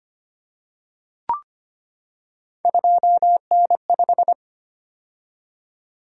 Morse code, heard as a keyed tone, about 25 wpm, 700 Hz. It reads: 2N5
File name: Three Letter-Number Combinations - 25wpm